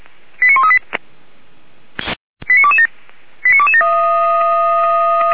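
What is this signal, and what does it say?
Five-tone selective call (ZVEI), heard as quick runs of five tones, tones of about 70 ms. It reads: 89138 89227 89227
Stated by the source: ZVEI five-tone call 89200 received by SDR, archived by feuerwehr-orlamuende.org fire brigade.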